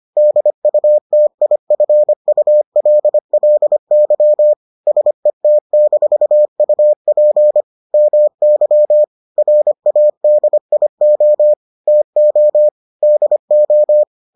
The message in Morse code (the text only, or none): DUTIFULLY SET-UP MY RADIO TO DO